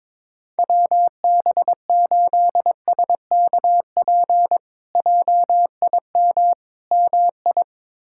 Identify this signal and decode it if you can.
Morse code, heard as a keyed tone, about 22 wpm, 700 Hz. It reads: WB8SKP JIM MI
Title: K1USN SST QSO Practice - 22wpm